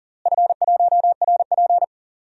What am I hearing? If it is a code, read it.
Morse code, heard as a keyed tone, about 40 wpm, 700 Hz. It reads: F1RP
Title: DXCC-100 Call Signs - 4 Characters - Repeated 3x - 40wpm